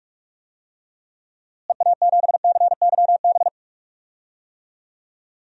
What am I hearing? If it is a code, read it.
Morse code, heard as a keyed tone, about 46 wpm, 700 Hz. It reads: EA7CYB
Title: DXCC-100 Call Signs - 6 Characters - Repeated 3x - 45wpm